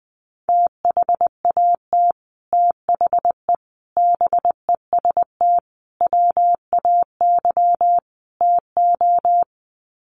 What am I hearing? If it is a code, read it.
Morse code, heard as a keyed tone, about 19 wpm, 700 Hz. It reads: THAT THE BEST WAY TO
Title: Sets of 5 Words Encore2 20wpm